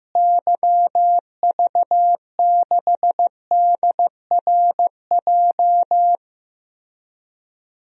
Morse code, heard as a keyed tone, about 15 wpm, 700 Hz. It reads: YV6DRJ
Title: DXCC-100 Call Signs - 6 Characters - Repeated 2x - 15wpm